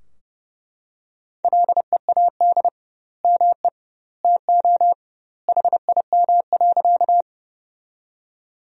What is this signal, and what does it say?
Morse code, heard as a keyed tone, about 30 wpm, 700 Hz. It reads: LEAD ME TO HIM.
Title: 2x Word-Spacing - Sentences from Top 500 Words - 30wpm